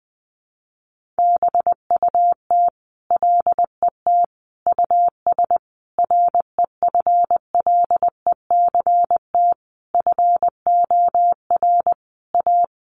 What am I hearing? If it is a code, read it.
Morse code, heard as a keyed tone, about 20 wpm, 700 Hz. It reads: BUT LET US REFLECT FOR A
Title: Sentences from Top 1500 Words 20wpm